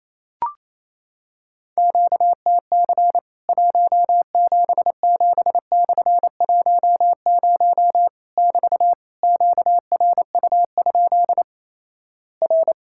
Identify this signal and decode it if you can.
Morse code, heard as a keyed tone, about 28 wpm, 700 Hz. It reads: QTC 177/10 <BT> QRU? R
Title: Worked All Europe DX Contest - QTC Exchange - 28wpm